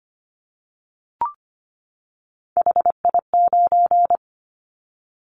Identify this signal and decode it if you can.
Morse code, heard as a keyed tone, about 25 wpm, 700 Hz. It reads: HI9
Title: Callsign Prefixes for Top 100 DXCC - 25wpm